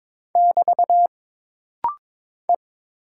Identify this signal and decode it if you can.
Morse code, heard as a keyed tone, about 22 wpm, 700 Hz. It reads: <BT> E
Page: This rapid-fire timing is geared to push your ICR (Instant Character Recognition) skills.